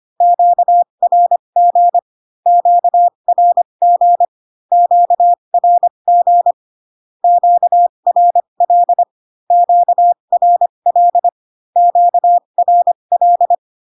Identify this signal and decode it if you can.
Morse code, heard as a keyed tone, about 25 wpm, spaced as for 22 wpm, 700 Hz. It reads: QRG QRG QRG QRL QRL QRL
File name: Q-codes most common repeated 3 times 18 25wpm